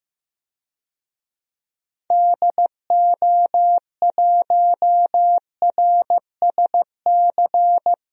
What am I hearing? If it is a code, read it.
Morse code, heard as a keyed tone, about 15 wpm, 700 Hz. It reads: DO1RSC
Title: DXCC-100 Call Signs - 6 Characters - Repeated 2x - 15wpm